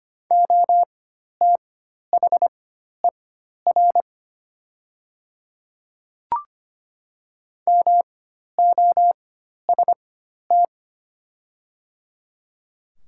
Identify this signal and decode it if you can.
Morse code, heard as a keyed tone, about 25 wpm, spaced as for 12 wpm, 700 Hz. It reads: OTHER MOST
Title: 4x Character Spacing - Top 100 Words - 25wpm